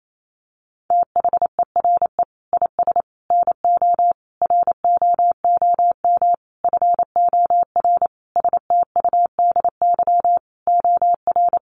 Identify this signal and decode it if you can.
Morse code, heard as a keyed tone, about 28 wpm, 700 Hz. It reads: THERE IS NO ROOM FOR STUDY OR
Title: Sentences from Top 1200 Words 28wpm